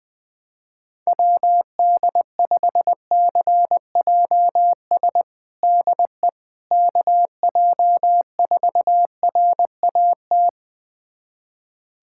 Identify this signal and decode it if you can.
Morse code, heard as a keyed tone, about 20 wpm, 700 Hz. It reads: WD5CJS DE KJ4RAT
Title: USA Callsign DE Callsign 20wpm